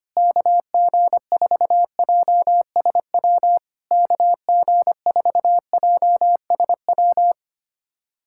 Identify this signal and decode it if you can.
Morse code, heard as a keyed tone, about 25 wpm, 700 Hz. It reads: KG4JSW KG4JSW
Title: SOTA QSO Practice 25wpm